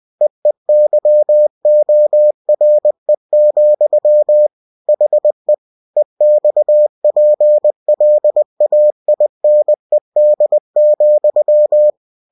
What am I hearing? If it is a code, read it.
Morse code, heard as a keyed tone, about 20 wpm, 600 Hz. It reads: EEYORE, HE EXPLAINED,
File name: cw-books / winnie-the-pooh / winnie-20wpm-page-0049